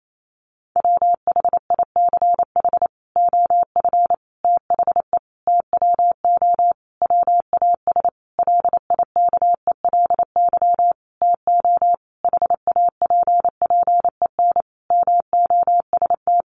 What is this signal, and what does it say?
Morse code, heard as a keyed tone, about 28 wpm, 700 Hz. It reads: WHICH OF THE TWO WAS LIKELY TO HAPPEN MOST
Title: Sentences from Top 1300 Words 28wpm